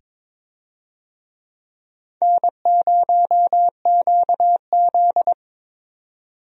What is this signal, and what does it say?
Morse code, heard as a keyed tone, about 23 wpm, 700 Hz. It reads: N0QZ
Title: USA Call Signs - 4 Characters - 22wpm